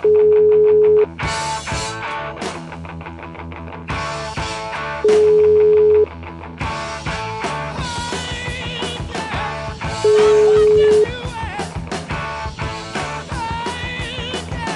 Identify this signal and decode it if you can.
Ringback tone (Europe, 425 Hz), with noise.